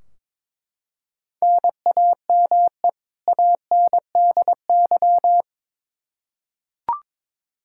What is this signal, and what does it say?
Morse code, heard as a keyed tone, about 22 wpm, 700 Hz. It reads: NAME ANDY
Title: Practice QSO - Exchange 1 - Beginning - 22wpm